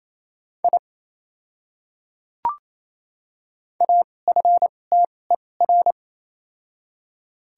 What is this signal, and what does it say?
Morse code, heard as a keyed tone, about 28 wpm, spaced as for 20 wpm, 700 Hz. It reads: I AFTER
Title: 2x Character Spacing - Top 300 Words - 28wpm